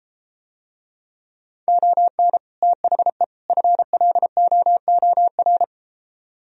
Morse code, heard as a keyed tone, about 33 wpm, 700 Hz. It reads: ON THE FLOOR